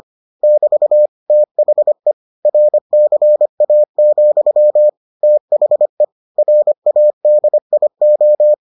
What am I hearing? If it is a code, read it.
Morse code, heard as a keyed tone, about 25 wpm, 600 Hz. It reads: <BT> THE RCA, THE RADIO